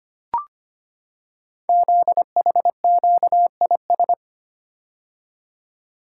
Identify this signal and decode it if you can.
Morse code, heard as a keyed tone, about 25 wpm, 700 Hz. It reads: ZHQIS
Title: Code Groups - 5 Letters - 25wpm